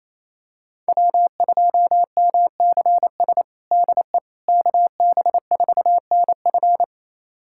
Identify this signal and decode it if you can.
Morse code, heard as a keyed tone, about 28 wpm, 700 Hz. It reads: W2MCS DE KB4NF
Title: USA Callsign DE Callsign - 5 Characters - 28wpm